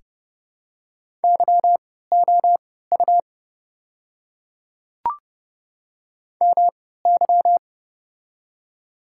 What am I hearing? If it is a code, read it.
Morse code, heard as a keyed tone, about 31 wpm, spaced as for 17 wpm, 700 Hz. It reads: YOU MY
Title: 3x Character Spacing - Top 300 Words - 30wpm